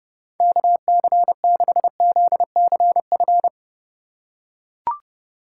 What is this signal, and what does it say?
Morse code, heard as a keyed tone, about 29 wpm, 700 Hz. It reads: KC6ZCF